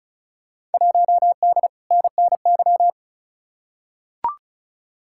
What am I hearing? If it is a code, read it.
Morse code, heard as a keyed tone, about 36 wpm, 700 Hz. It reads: 1D NNY
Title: Field Day Exchanges - 35wpm